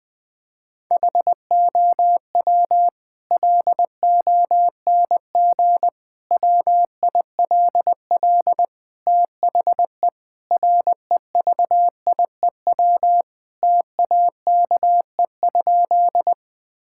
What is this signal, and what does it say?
Morse code, heard as a keyed tone, about 21 wpm, 700 Hz. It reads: HOW LONG WILL THE REVIEW TAKE?
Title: Sentences from Top 1000 Words - Review Part 6 - 20wpm